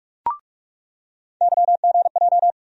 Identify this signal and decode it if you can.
Morse code, heard as a keyed tone, about 47 wpm, 700 Hz. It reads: YGJ
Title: USA Call Sign Postfixes - 3 Letters 45wpm